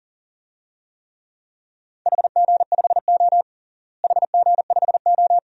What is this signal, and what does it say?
Morse code, heard as a keyed tone, about 40 wpm, 700 Hz. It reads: HG5O HG5O
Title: DXCC-100 Call Signs - 4 Characters - Repeated 2x - 40wpm